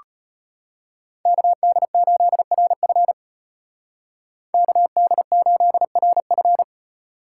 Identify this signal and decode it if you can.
Morse code, heard as a keyed tone, about 33 wpm, 700 Hz. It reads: KD8RF KD8RF